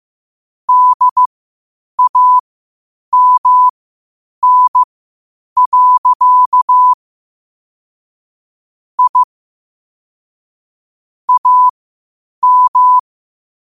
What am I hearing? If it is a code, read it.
Morse code, heard as a keyed tone, about 15 wpm, spaced as for 8 wpm, 1000 Hz. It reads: DAMN. I AM
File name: qotdfiles / 8wpm